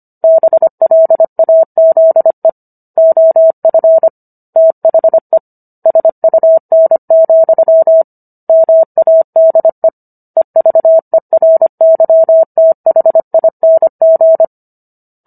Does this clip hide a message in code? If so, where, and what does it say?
Yes: Morse code from 0.2 to 14.4 s, heard as a keyed tone, about 25 wpm, 650 Hz. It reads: BLAZE OF THE SUN, MADE EVERYTHING